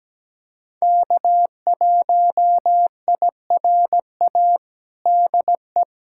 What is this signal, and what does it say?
Morse code, heard as a keyed tone, about 17 wpm, 700 Hz. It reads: K1IRA DE
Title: USA Callsign DE Callsign - 5 Characters - 17wpm